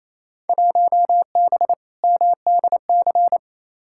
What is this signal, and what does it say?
Morse code, heard as a keyed tone, about 28 wpm, 700 Hz. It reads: 1B MDC
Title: Field Day Exchanges Encore - 28wpm